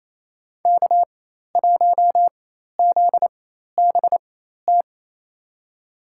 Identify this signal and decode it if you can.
Morse code, heard as a keyed tone, about 28 wpm, spaced as for 13 wpm, 690 Hz. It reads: K1ZBT
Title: USA Call Signs - 5 Characters - 4x Character Spacing - 28wpm